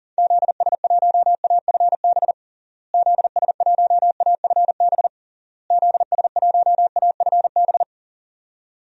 Morse code, heard as a keyed tone, about 40 wpm, 700 Hz. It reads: ZS1AFB ZS1AFB ZS1AFB
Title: DXCC-100 Call Signs - 6 Characters - Repeated 3x - 40wpm